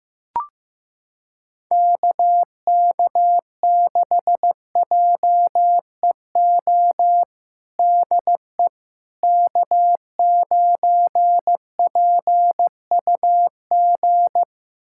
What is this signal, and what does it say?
Morse code, heard as a keyed tone, about 15 wpm, 700 Hz. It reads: KK6JEO DE K9PUG
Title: USA Callsign DE Callsign 15wpm